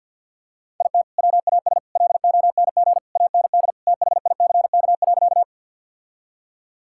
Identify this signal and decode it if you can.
Morse code, heard as a keyed tone, about 50 wpm, 700 Hz. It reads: IT WAS LONG AND THICK.
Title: Sentences from Top 1600 Words 50wpm